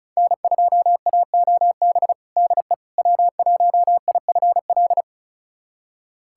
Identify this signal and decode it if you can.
Morse code, heard as a keyed tone, about 36 wpm, 690 Hz. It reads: N2AOB DE W1IFL
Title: USA Callsign DE Callsign - 5 Characters - 35wpm